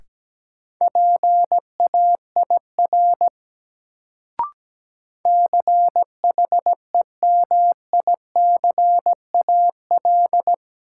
Morse code, heard as a keyed tone, about 17 wpm, 700 Hz. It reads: PAIR CHEMICAL
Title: Sentences from Top 1200 Words - New Word Drill - 17wpm